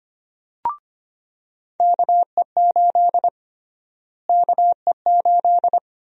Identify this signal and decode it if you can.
Morse code, heard as a keyed tone, about 25 wpm, 700 Hz. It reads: KE8 KE8
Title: USA Call Sign Prefixes - Repeated 2x - 25wpm